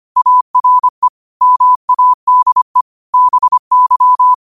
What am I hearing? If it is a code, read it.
Morse code, heard as a keyed tone, about 25 wpm, 1000 Hz. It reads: ARE MADE BY
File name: qotdfiles / 25wpm